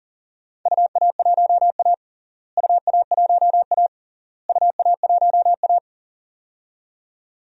Morse code, heard as a keyed tone, about 40 wpm, 700 Hz. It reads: UA1A UA1A UA1A
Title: DXCC-100 Call Signs - 4 Characters - Repeated 3x - 40wpm